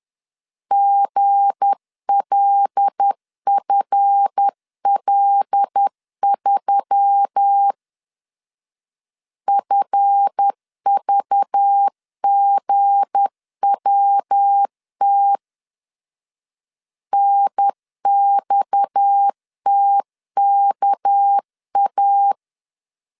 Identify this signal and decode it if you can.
Morse code, heard as a keyed tone, about 10 wpm, 800 Hz. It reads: GLFL3 FVGWT NXTKA
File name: LEZIONI / 7 GF9Z / 7 GF 10wpm 2514